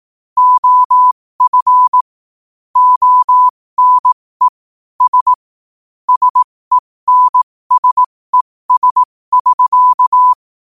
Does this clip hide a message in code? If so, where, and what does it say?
Yes: Morse code from 0.4 to 10.3 s, heard as a keyed tone, about 18 wpm, spaced as for 15 wpm, 1000 Hz. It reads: OF ONE S SENSES<SK>